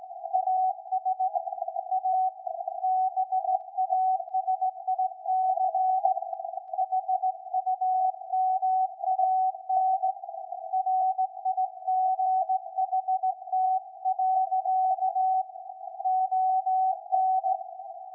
Morse code, heard as a keyed tone, about 18 wpm, 750 Hz. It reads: AS A BASIC HUMAN RIGHT. ON
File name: iz7slz / VLF / SAQ 13-02-2023